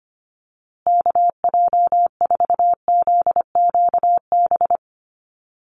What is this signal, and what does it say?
Morse code, heard as a keyed tone, about 25 wpm, 700 Hz. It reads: KJ4ZQB
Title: USA Call Signs Repeated 2x 25wpm